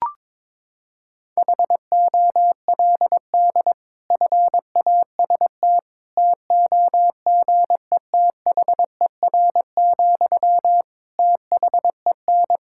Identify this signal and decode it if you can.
Morse code, heard as a keyed tone, about 22 wpm, 700 Hz. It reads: HOLD FAST TOGETHER, THEN
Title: Sentences from Top 1600 Words 22wpm